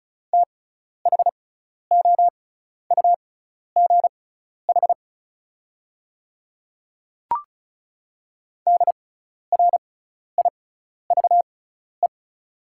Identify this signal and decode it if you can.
Morse code, heard as a keyed tone, about 34 wpm, spaced as for 12 wpm, 690 Hz. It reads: THOUGH DRIVE
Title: 6x Character Spacing - Top 500 Words - 35wpm